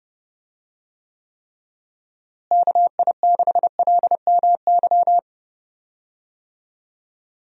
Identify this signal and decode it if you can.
Morse code, heard as a keyed tone, about 30 wpm, 700 Hz. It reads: KI6LMY
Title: USA Call Signs - 6 Characters - Repeated 3x - 30wpm